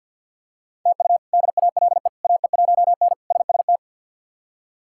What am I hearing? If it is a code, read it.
Morse code, heard as a keyed tone, about 48 wpm, 700 Hz. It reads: TU DALE AE1N SST
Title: K1USN SST QSO Practice - 50wpm